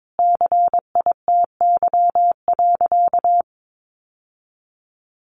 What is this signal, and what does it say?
Morse code, heard as a keyed tone, about 22 wpm, 700 Hz. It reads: CITY.
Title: Sentences from Top 1200 Words 22wpm